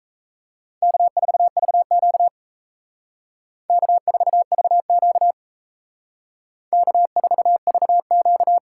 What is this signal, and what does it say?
Morse code, heard as a keyed tone, about 35 wpm, 700 Hz. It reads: K4VQ K4VQ K4VQ
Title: Speed Racing - USA 1x2 Call Signs 28wpm